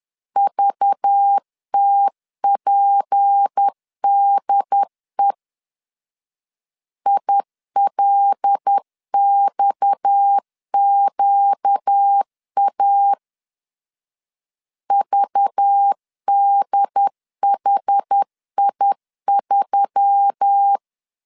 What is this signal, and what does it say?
Morse code, heard as a keyed tone, about 10 wpm, 800 Hz. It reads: VTPDE ILXQA VDHI3
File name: LEZIONI / 6 PKX0 / 6 PKX 10wpm 2426